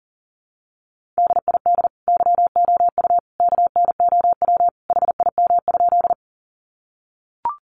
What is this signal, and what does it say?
Morse code, heard as a keyed tone, about 40 wpm, 700 Hz. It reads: DID YOU KNOW HIM?
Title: Sentences from Top 100 Words 40wpm